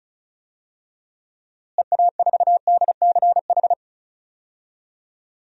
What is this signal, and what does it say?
Morse code, heard as a keyed tone, about 36 wpm, 700 Hz. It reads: EA4DCH
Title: DXCC-100 Call Signs - 6 Characters - Repeated 2x - 35wpm